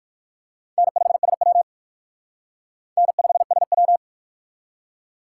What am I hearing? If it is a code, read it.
Morse code, heard as a keyed tone, about 51 wpm, 700 Hz. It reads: N5SW N5SW
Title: Speed Racing - USA 1x2 Call Signs 45wpm